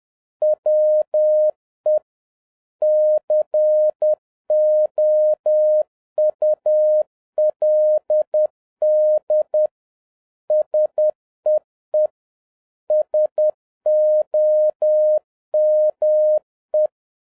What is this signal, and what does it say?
Morse code, heard as a keyed tone, about 10 wpm, 610 Hz. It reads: WE COULD SEE SOME
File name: 2011-07-mp3 / n36 THE 3.11 2 10wpm